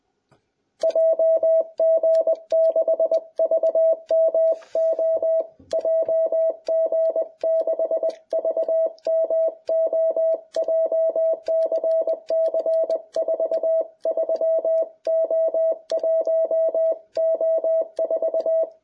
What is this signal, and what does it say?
Morse code, heard as a keyed tone, about 22 wpm, 660 Hz. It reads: JG6VMO JG6VMO JCC 43O1O4